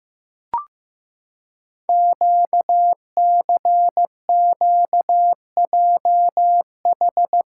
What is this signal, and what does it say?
Morse code, heard as a keyed tone, about 15 wpm, 700 Hz. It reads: QCQJH
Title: Code Groups - 5 Letters - 15wpm